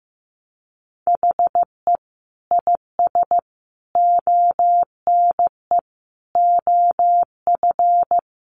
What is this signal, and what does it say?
Morse code, heard as a keyed tone, about 15 wpm, 700 Hz. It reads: HE IS ONE OF
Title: Sets of 5 Words Encore 15wpm